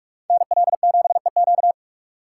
Morse code, heard as a keyed tone, about 42 wpm, 700 Hz. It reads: NL7EQ